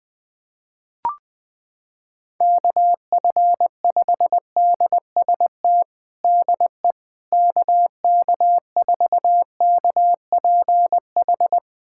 Morse code, heard as a keyed tone, about 21 wpm, 700 Hz. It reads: KF5DST DE KK4KPH
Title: USA Callsign DE Callsign 20wpm